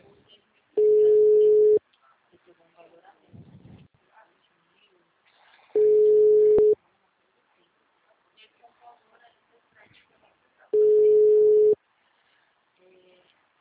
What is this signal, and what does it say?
Ringback tone (Europe, 425 Hz), with speech.
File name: out- -2018- - - .81152